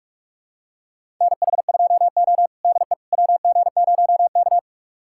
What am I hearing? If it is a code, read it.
Morse code, heard as a keyed tone, about 46 wpm, 700 Hz. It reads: NH2O DE WG0K
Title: USA Callsign DE Callsign - 4 Characters - 45wpm